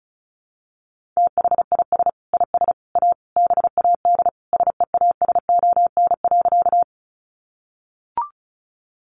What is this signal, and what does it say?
Morse code, heard as a keyed tone, about 34 wpm, 700 Hz. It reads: THIS IS A BAD SEASON.
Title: Sentences from Top 1400 Words 35wpm